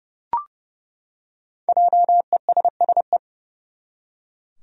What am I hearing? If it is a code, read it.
Morse code, heard as a keyed tone, about 30 wpm, 700 Hz. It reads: JESSE
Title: USA Common Names - 30wpm